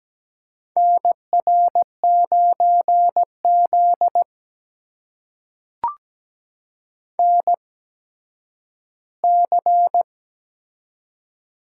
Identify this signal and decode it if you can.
Morse code, heard as a keyed tone, about 17 wpm, spaced as for 10 wpm, 700 Hz. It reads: NR9Z NC
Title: USA Call Signs - 4 Characters - 8x Character Spacing - 17wpm